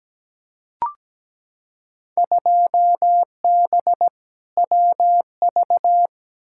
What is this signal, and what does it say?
Morse code, heard as a keyed tone, about 17 wpm, 700 Hz. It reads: 2B WV
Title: Field Day Exchanges Encore - 17wpm